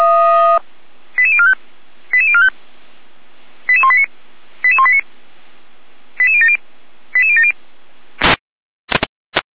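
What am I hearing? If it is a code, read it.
Five-tone selective call (ZVEI), heard as quick runs of five tones, tones of about 80 ms. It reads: 88045 88045 88189 88189 88070 88070
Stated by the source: ZVEI five-tone call received by SDR, from the feuerwehr-orlamuende.org fire brigade audio server.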